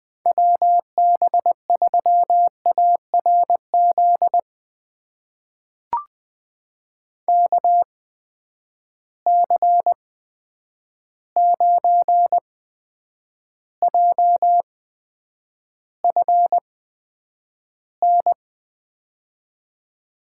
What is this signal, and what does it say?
Morse code, heard as a keyed tone, about 20 wpm, spaced as for 9 wpm, 700 Hz. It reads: WB3ARZ KC9JFN
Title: USA Call Signs - 6 Characters - 8x Character Spacing - 20wpm